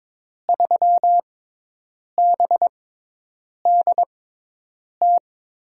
Morse code, heard as a keyed tone, about 22 wpm, spaced as for 8 wpm, 700 Hz. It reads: 3BDT
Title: USA Call Signs - 6x Character Spacing - 22wpm